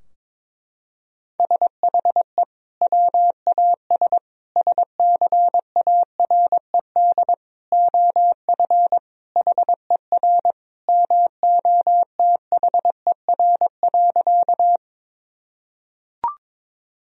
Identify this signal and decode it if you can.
Morse code, heard as a keyed tone, about 22 wpm, 700 Hz. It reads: SHE WAS SCARED OF HER MOTHER.
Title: Sentences from Top 1000 Words - Review Part 4 - 22wpm